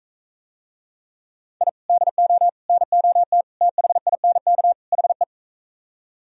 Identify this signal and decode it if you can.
Morse code, heard as a keyed tone, about 45 wpm, 700 Hz. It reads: I DO NOT THINK HE